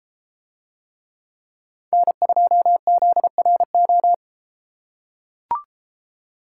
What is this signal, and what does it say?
Morse code, heard as a keyed tone, about 32 wpm, 700 Hz. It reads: N2ZRO